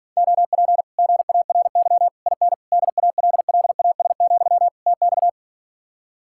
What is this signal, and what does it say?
Morse code, heard as a keyed tone, about 52 wpm, 700 Hz. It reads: OP GARY IN DALLAS, TX